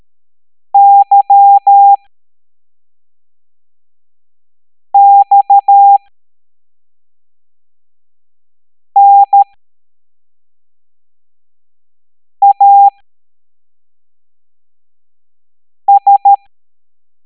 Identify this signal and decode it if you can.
Morse code, heard as a keyed tone, about 13 wpm, spaced as for 3 wpm, 800 Hz. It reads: YXNAS
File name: hamdata / L05-EX01-SVY